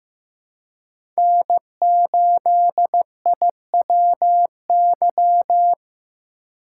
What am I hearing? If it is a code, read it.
Morse code, heard as a keyed tone, about 18 wpm, 700 Hz. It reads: N8IWY
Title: Speed Racing - USA 1x3 Call Signs 15wpm